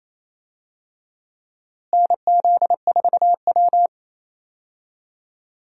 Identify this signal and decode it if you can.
Morse code, heard as a keyed tone, about 28 wpm, 700 Hz. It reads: NZ4W
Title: USA Call Signs - 4 Characters - Repeated 3x - 28wpm